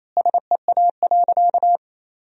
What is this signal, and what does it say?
Morse code, heard as a keyed tone, about 28 wpm, 700 Hz. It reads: SEA.
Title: 2x Word-Spacing - Sentences from Top 400 Words - 28wpm